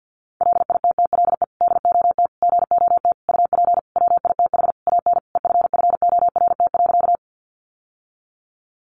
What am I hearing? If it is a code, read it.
Morse code, heard as a keyed tone, about 50 wpm, 700 Hz. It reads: LITTLE DOT GOT UP WITH AN EFFORT.
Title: Sentences from Top 1500 Words 50wpm